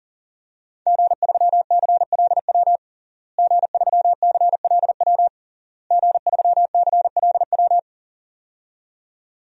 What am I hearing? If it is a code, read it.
Morse code, heard as a keyed tone, about 40 wpm, 700 Hz. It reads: G3CLW G3CLW G3CLW
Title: DXCC-100 Call Signs - 5 Characters - Repeated 3x - 40wpm